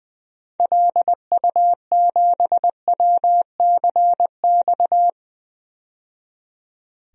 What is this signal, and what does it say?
Morse code, heard as a keyed tone, about 20 wpm, 700 Hz. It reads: LU7WCX
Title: DXCC-100 Call Signs - 6 Characters - 20wpm